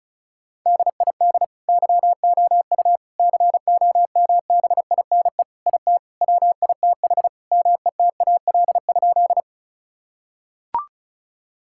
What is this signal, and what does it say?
Morse code, heard as a keyed tone, about 34 wpm, 700 Hz. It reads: DID YOU COMBINE IT WITH METAL?
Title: Sentences from Top 1600 Words 35wpm